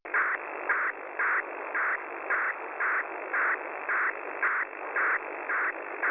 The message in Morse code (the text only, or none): none